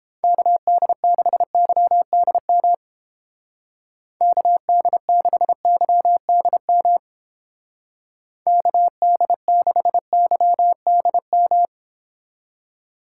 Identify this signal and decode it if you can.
Morse code, heard as a keyed tone, about 29 wpm, 700 Hz. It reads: KD6YDM KD6YDM KD6YDM